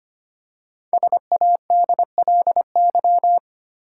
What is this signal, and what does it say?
Morse code, heard as a keyed tone, about 25 wpm, 700 Hz. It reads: SADLY
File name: 5 Letter Words - First 3 Letters Given - 25wpm